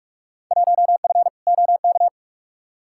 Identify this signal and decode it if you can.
Morse code, heard as a keyed tone, about 47 wpm, 700 Hz. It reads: 1F OK
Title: Field Day Exchanges Encore - 45wpm